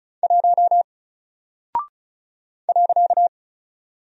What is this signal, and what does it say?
Morse code, heard as a keyed tone, about 36 wpm, 700 Hz. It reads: 1 .
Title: Single Character (Letter:Number:Punctuation:Prosign) Alt 35wpm